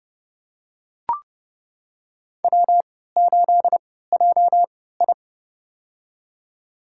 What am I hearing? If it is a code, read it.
Morse code, heard as a keyed tone, about 30 wpm, spaced as for 17 wpm, 700 Hz. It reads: W8JI
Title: USA Call Signs - 4 Characters - 3x Character Spacing - 30wpm